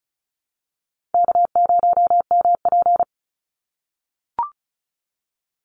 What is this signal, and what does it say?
Morse code, heard as a keyed tone, about 34 wpm, 700 Hz. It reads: K0MP
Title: NA SOTA Chaser Callsigns 35wpm